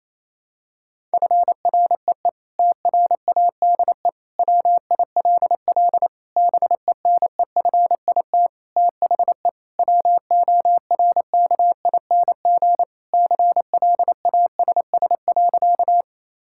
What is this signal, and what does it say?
Morse code, heard as a keyed tone, about 28 wpm, 700 Hz. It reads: FREE TRADE WILL BENEFIT THE WORKING CLASS.
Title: Sentences from Top 1000 Words - Review Part 6 - 28wpm